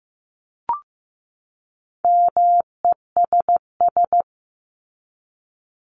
Morse code, heard as a keyed tone, about 15 wpm, 700 Hz. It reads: MESS
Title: Top 1751-2000 Words 15wpm